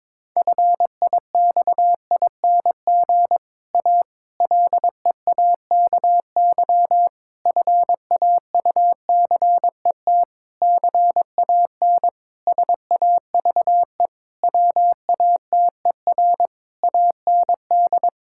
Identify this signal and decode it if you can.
Morse code, heard as a keyed tone, about 22 wpm, 700 Hz. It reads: FIXING A LEAKY FAUCET CAN SAVE WATER AND
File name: Difficult Sentences with Hints - 22wpm